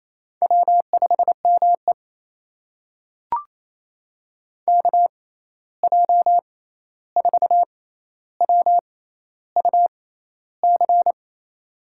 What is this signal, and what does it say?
Morse code, heard as a keyed tone, about 28 wpm, spaced as for 13 wpm, 700 Hz. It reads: W5ME KJ4WUC
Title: USA Call Signs - 6x Character Spacing - 28wpm